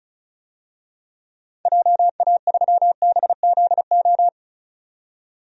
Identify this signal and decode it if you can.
Morse code, heard as a keyed tone, about 34 wpm, 700 Hz. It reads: JA3BZO
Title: DXCC-100 Call Signs - 6 Characters - 35wpm